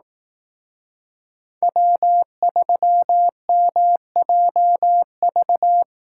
Morse code, heard as a keyed tone, about 18 wpm, 700 Hz. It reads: W3MJV
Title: Speed Racing - USA 1x3 Call Signs 15wpm